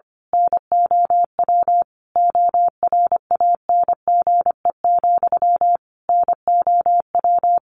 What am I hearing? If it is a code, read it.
Morse code, heard as a keyed tone, about 25 wpm, 700 Hz. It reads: NOW ORANGE, NOW
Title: A Sound Of Thunder 25wpm